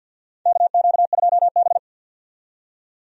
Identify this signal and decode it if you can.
Morse code, heard as a keyed tone, about 53 wpm, 700 Hz. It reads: KQ2B
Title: USA Call Signs - 4 Characters - 50wpm